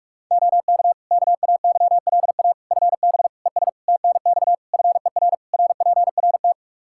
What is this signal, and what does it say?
Morse code, heard as a keyed tone, about 45 wpm, 700 Hz. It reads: OK KAYLA FB ES TNX FER RPRT
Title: K1USN SST QSO Practice 45wpm